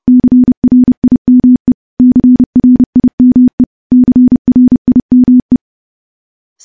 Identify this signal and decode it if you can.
Morse code, heard as a keyed tone, about 30 wpm, 250 Hz. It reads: CRIME CRIME CRIME
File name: morse-training-30WPM